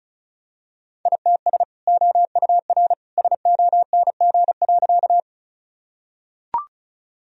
Morse code, heard as a keyed tone, about 36 wpm, 700 Hz. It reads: ITS OUR SONG.
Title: Sentences from Top 1000 Words - Review Part 2 - 35wpm